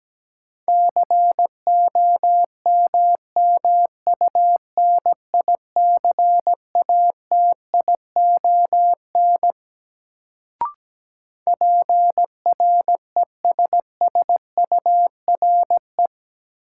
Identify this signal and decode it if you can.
Morse code, heard as a keyed tone, about 17 wpm, 700 Hz. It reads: COMMUNICATION PRESSURE
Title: Sentences from Top 1400 Words - New Word Drill Alt - 17wpm